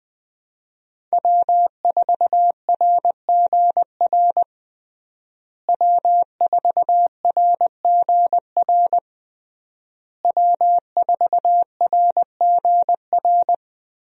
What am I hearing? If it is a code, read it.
Morse code, heard as a keyed tone, about 20 wpm, 700 Hz. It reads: W4RGR W4RGR W4RGR
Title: USA Call Signs - 5 Characters - Repeated 3x - 20wpm